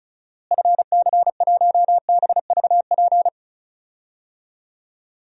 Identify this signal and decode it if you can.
Morse code, heard as a keyed tone, about 34 wpm, 700 Hz. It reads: FC1BVP
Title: DXCC-100 Call Signs - 6 Characters - Repeated 3x - 35wpm